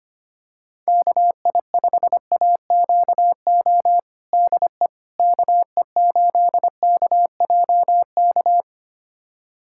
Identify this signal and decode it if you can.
Morse code, heard as a keyed tone, about 25 wpm, 700 Hz. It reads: KI5AQO DE KE8KJK
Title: USA Callsign DE Callsign 25wpm